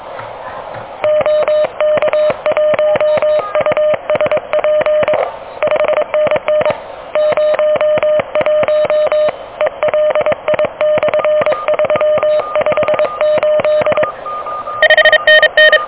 Morse code, heard as a keyed tone, about 24 wpm, 610 Hz, with radio noise. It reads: OK1USP 5NN 01 ELI/358 5NN
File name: OK QRP 2023 / OK QRP 2023